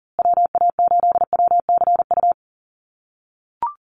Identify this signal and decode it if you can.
Morse code, heard as a keyed tone, about 40 wpm, 700 Hz. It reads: WA8WCU
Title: USA Call Signs Repeated 3x 40wpm